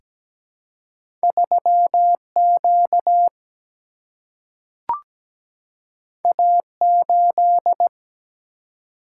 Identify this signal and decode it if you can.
Morse code, heard as a keyed tone, about 17 wpm, 700 Hz. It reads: 3 Q A 8
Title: Single-Letter Single-Number 17wpm